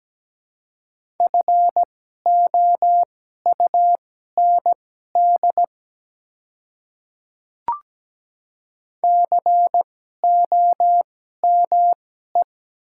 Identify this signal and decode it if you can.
Morse code, heard as a keyed tone, about 17 wpm, spaced as for 12 wpm, 700 Hz. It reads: FOUND COME
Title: 2x Character Spacing - Top 300 Words - 17wpm